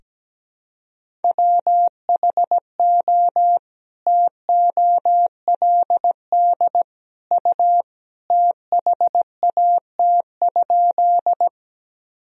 Morse code, heard as a keyed tone, about 17 wpm, 700 Hz. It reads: WHO TOLD U THAT?
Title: Sentences from Top 400 Words and Abbreviations 17wpm